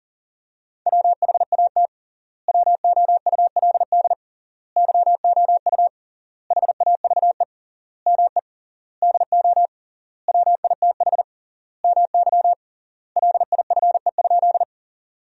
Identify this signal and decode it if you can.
Morse code, heard as a keyed tone, about 40 wpm, 700 Hz. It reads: WHAT WOULD YOU HAVE ME DO WITH MY LIFE?
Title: 3x Word-Spacing - Sentences from Top 300 Words - 40wpm